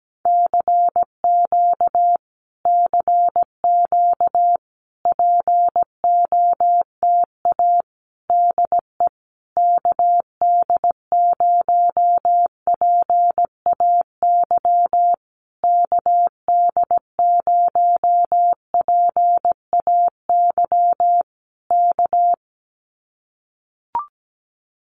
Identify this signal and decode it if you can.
Morse code, heard as a keyed tone, about 17 wpm, 700 Hz. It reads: CQ CQ POTA DE KD0PAY KD0PAY K
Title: Common QSO Phrases Encore 17wpm